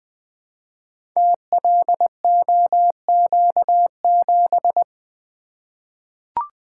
Morse code, heard as a keyed tone, about 20 wpm, 700 Hz. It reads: TLOQ7
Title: Code Groups - 5 Letter:Numbers - 20wpm